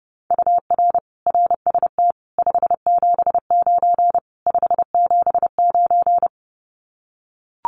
Morse code, heard as a keyed tone, about 30 wpm, 700 Hz. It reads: UR RST 579 579
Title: Common QSO Phrases Encore 30wpm